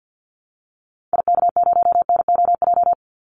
Morse code, heard as a keyed tone, about 49 wpm, 700 Hz. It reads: IK0NOJ
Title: Call Signs Repeated 2x - 50wpm